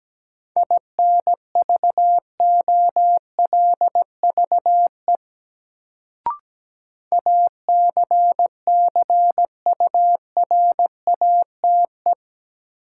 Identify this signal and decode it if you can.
Morse code, heard as a keyed tone, about 17 wpm, 700 Hz. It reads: INVOLVE ACCURATE
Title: Sentences from Top 1200 Words - New Word Drill Alt - 17wpm